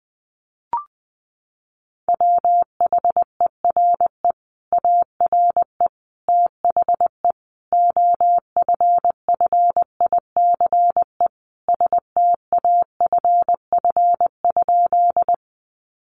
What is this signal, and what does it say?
Morse code, heard as a keyed tone, about 19 wpm, 700 Hz. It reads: WHERE ARE THE OFFICE STAFF?
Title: Sentences from Top 1200 Words 20wpm